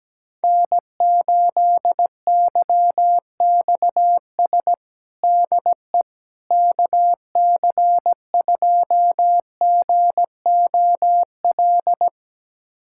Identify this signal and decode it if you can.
Morse code, heard as a keyed tone, about 17 wpm, 700 Hz. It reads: N8YXS DE KC2GOL
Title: USA Callsign DE Callsign 17wpm